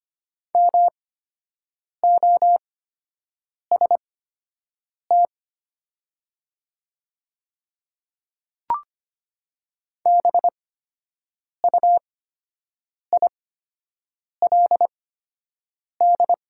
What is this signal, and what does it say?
Morse code, heard as a keyed tone, about 25 wpm, spaced as for 7 wpm, 700 Hz. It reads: MOST BUILD
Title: 8x Character Spacing - Top 200 Words - 25wpm